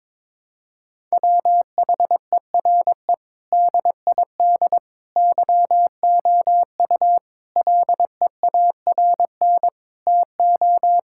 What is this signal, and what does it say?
Morse code, heard as a keyed tone, about 22 wpm, 700 Hz. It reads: WHERE DID YOU LEARN TO
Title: Sentences from Top 1000 Words - Review Part 6 - 22wpm